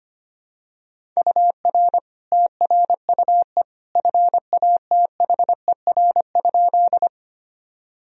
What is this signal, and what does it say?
Morse code, heard as a keyed tone, about 25 wpm, 700 Hz. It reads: UR TRUE FATHER?
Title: Sentences from Top 400 Words and Abbreviations 25wpm